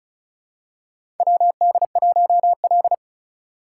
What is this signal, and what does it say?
Morse code, heard as a keyed tone, about 36 wpm, 700 Hz. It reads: WD1L
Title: USA Call Signs - 4 Characters - 2x Character Spacing - 35wpm